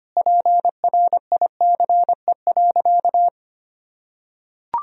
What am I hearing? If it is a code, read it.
Morse code, heard as a keyed tone, about 25 wpm, 700 Hz. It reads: PRICE.
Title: Sentences from Top 1000 Words - Review Part 4 - 25wpm